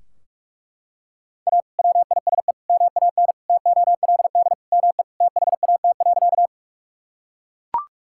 Morse code, heard as a keyed tone, about 41 wpm, 700 Hz. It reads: A WISE MAN TOLD ME THAT.
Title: Sentences from Top 1600 Words 45wpm